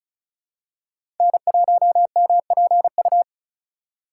Morse code, heard as a keyed tone, about 36 wpm, 700 Hz. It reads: N1MPU
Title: USA Call Signs - 5 Characters - 2x Character Spacing - 35wpm